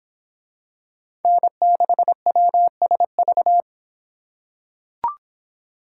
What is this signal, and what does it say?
Morse code, heard as a keyed tone, about 23 wpm, 700 Hz. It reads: N6WSV